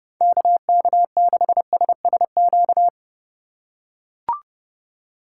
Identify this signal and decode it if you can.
Morse code, heard as a keyed tone, about 30 wpm, 700 Hz. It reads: KK6SSQ
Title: USA Call Signs - 6 Characters - 2x Character Spacing - 30wpm